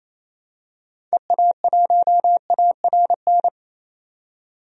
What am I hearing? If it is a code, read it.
Morse code, heard as a keyed tone, about 28 wpm, 700 Hz. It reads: EA1ARN
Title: DXCC-100 Call Signs - 6 Characters - 28wpm